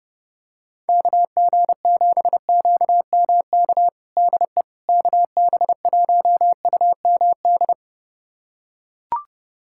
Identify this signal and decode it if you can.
Morse code, heard as a keyed tone, about 30 wpm, 700 Hz. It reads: KG7QMK DE KB1UMD
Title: USA Callsign DE Callsign - 6 Characters - 30wpm